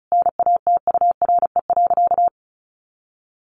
Morse code, heard as a keyed tone, about 34 wpm, 700 Hz. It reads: NATURE.
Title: Sentences from Top 1500 Words 35wpm